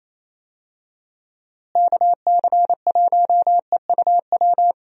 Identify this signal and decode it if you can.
Morse code, heard as a keyed tone, about 27 wpm, 700 Hz. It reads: KC1EUW
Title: USA Call Signs - 6 Characters - Repeated 3x - 28wpm